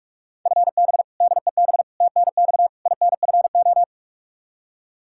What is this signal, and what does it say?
Morse code, heard as a keyed tone, about 41 wpm, 700 Hz. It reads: FB DEB TNX INFO
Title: Practice QSO - Exchange 3 - More Info - 45wpm